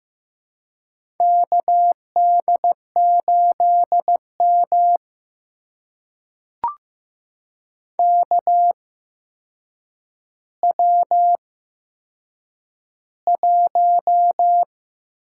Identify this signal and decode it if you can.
Morse code, heard as a keyed tone, about 15 wpm, spaced as for 7 wpm, 700 Hz. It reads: KD8M KW1
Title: USA Call Signs - 4 Characters - 8x Character Spacing - 15wpm